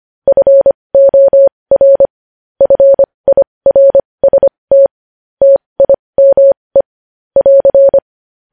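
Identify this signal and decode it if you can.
Morse code, heard as a keyed tone, about 25 wpm, spaced as for 20 wpm, 550 Hz. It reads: FOR FIRST TIME <AR>
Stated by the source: Morse code news headlines practice at 20.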